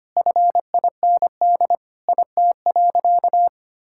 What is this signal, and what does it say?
Morse code, heard as a keyed tone, about 25 wpm, 700 Hz. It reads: FIND IT.
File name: practice / 25wpm / Sentences with 3-Word Phrases - 25wpm